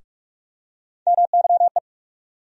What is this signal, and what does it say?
Morse code, heard as a keyed tone, about 46 wpm, 700 Hz. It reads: MYE
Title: USA Call Sign Postfixes - 3 Letters 45wpm